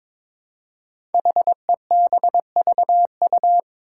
Morse code, heard as a keyed tone, about 22 wpm, 700 Hz. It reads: HEBVU
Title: Five Challenging Letter-Number Combinations 22wpm